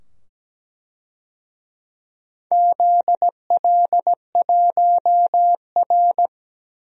Morse code, heard as a keyed tone, about 17 wpm, 700 Hz. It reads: ZL1R
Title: DXCC-100 Call Signs - 4 Characters - 17wpm